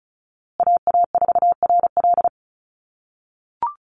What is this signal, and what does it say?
Morse code, heard as a keyed tone, about 34 wpm, 700 Hz. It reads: AA4RL
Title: USA Call Signs Repeated 2x 35wpm